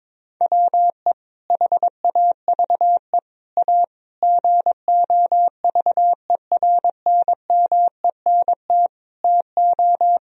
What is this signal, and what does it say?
Morse code, heard as a keyed tone, about 22 wpm, 700 Hz. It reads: WE HAVE A GOVERNMENT TO
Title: Sentences from Top 1000 Words - Review Part 1 - 22wpm